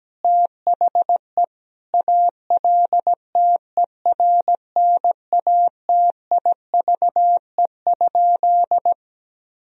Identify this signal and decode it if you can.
Morse code, heard as a keyed tone, about 17 wpm, 700 Hz. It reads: THE ALTERNATIVE?
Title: Sentences from Top 1000 Words - Review Part 4 - 17wpm